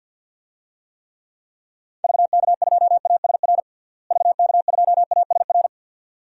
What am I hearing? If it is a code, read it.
Morse code, heard as a keyed tone, about 51 wpm, 700 Hz. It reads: VK2ASR VK2ASR
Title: DXCC-100 Call Signs - 6 Characters - Repeated 2x - 50wpm